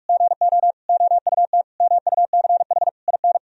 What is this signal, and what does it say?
Morse code, heard as a keyed tone, about 46 wpm, 700 Hz. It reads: GO OUT MUCH IN
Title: Sentences from Top 1000 Words - Review Part 4 - 45wpm